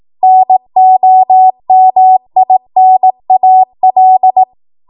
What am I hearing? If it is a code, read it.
Morse code, heard as a keyed tone, about 18 wpm, 750 Hz. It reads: NOMINAL
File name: Morse / 18WPM